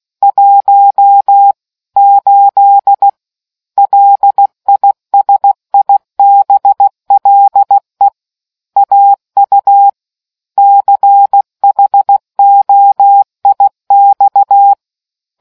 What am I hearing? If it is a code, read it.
Morse code, heard as a keyed tone, about 16 wpm, 800 Hz. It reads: 1 8 LISIBLE AU CHOIX